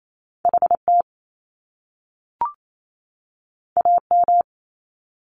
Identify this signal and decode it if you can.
Morse code, heard as a keyed tone, about 28 wpm, 700 Hz. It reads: HT AM
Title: Top Two Letter Combinations in Words 28wpm